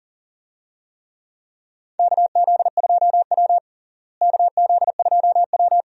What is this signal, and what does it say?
Morse code, heard as a keyed tone, about 40 wpm, 700 Hz. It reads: KZ2W KZ2W
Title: USA Call Signs - 4 Characters - Repeated 3x - 40wpm